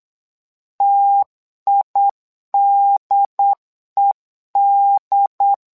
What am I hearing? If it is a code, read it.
Morse code, heard as a keyed tone, about 8 wpm, 800 Hz. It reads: TIDED
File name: LEZIONI / 2 UHDN / 2 UHD 8wpm 2258